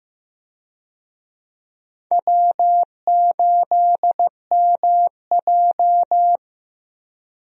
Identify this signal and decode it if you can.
Morse code, heard as a keyed tone, about 15 wpm, 700 Hz. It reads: W8MJ
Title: USA Call Signs - 4 Characters - 15wpm